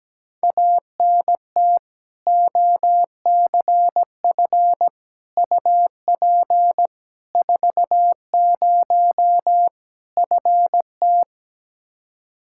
Morse code, heard as a keyed tone, about 17 wpm, 700 Hz. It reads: ANT OCF UP 40 FT
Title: K1USN SST QSO Practice 17wpm